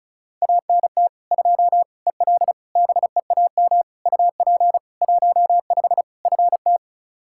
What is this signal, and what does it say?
Morse code, heard as a keyed tone, about 34 wpm, 700 Hz. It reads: ANT 2 EL BEAM UP 15 FT
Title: Practice QSO - Exchange 2 - Info - 35wpm